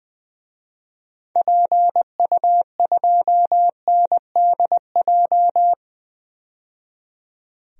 Morse code, heard as a keyed tone, about 21 wpm, 700 Hz. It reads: PU2NDJ
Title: DXCC-100 Call Signs - 6 Characters - 20wpm